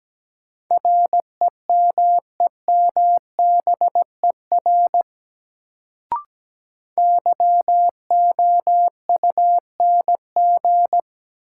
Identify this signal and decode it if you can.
Morse code, heard as a keyed tone, about 17 wpm, 700 Hz. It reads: REMEMBER YOUNG
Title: Sentences from Top 400 Words - New Word Drill - 17wpm